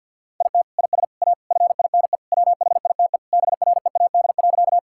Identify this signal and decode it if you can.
Morse code, heard as a keyed tone, about 50 wpm, 700 Hz. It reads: IT IS A FINE WHITE BREAD.
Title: Sentences from Top 1600 Words 50wpm